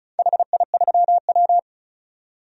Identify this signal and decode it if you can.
Morse code, heard as a keyed tone, about 36 wpm, 690 Hz. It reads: HI3W
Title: DXCC-100 Call Signs - 4 Characters - 35wpm